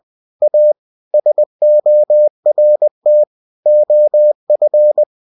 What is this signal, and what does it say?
Morse code, heard as a keyed tone, about 20 wpm, 600 Hz. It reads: A SORT OF
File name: cw-books / winnie-the-pooh / winnie-20wpm-page-0082